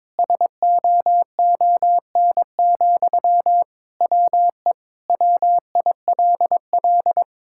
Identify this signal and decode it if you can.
Morse code, heard as a keyed tone, about 22 wpm, 700 Hz. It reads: SOON, WE WILL
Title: Sentences from Top 1000 Words - Review Part 6 - 22wpm